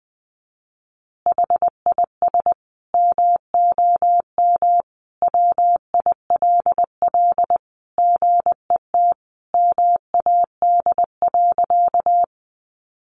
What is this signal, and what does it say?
Morse code, heard as a keyed tone, about 20 wpm, 700 Hz. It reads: HIS MOM WILL GET MAD.
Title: Additional Easy Phrases 20wpm